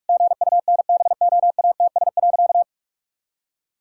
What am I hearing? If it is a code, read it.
Morse code, heard as a keyed tone, about 46 wpm, 700 Hz. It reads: GUNBOATS.
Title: Pangram Challenge v2 - 45wpm